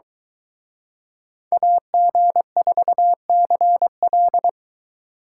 Morse code, heard as a keyed tone, about 23 wpm, 700 Hz. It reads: AG4CL